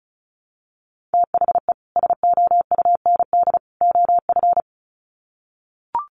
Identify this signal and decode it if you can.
Morse code, heard as a keyed tone, about 34 wpm, 700 Hz. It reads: THE SOUND OF
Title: Sets of 3 Words 35wpm